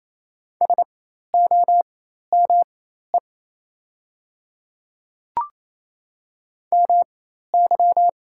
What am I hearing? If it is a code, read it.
Morse code, heard as a keyed tone, about 28 wpm, spaced as for 13 wpm, 690 Hz. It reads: SOME MY
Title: 4x Character Spacing - Top 200 Words - 28wpm